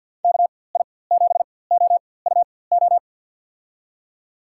Morse code, heard as a keyed tone, about 51 wpm, spaced as for 23 wpm, 700 Hz. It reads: KI7OUO
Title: USA Call Signs - 4x Character Spacing - 50wpm